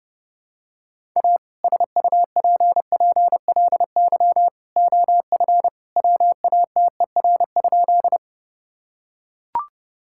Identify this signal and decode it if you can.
Morse code, heard as a keyed tone, about 32 wpm, 700 Hz. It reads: A SUPPLY OF WATER?
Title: Sentences from Top 1000 Words - Review Part 5 - 30wpm